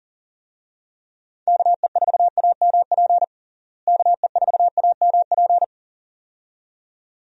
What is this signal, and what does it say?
Morse code, heard as a keyed tone, about 40 wpm, 700 Hz. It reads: KE4AMP KE4AMP
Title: USA Call Signs - 6 Characters - Repeated 2x - 40wpm